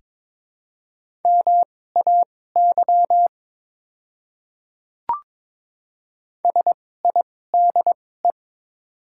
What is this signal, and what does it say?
Morse code, heard as a keyed tone, about 22 wpm, spaced as for 16 wpm, 700 Hz. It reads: MAY SIDE
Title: 2x Character Spacing - Top 300 Words - 22wpm